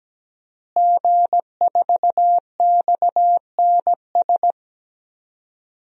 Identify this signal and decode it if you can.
Morse code, heard as a keyed tone, about 17 wpm, 700 Hz. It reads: G4XNS
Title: DXCC-100 Call Signs - 5 Characters - Repeated 2x - 17wpm